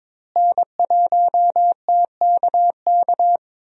Morse code, heard as a keyed tone, about 22 wpm, 700 Hz. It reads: N1TKK
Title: USA Call Signs - 5 Characters - Repeated 3x - 22wpm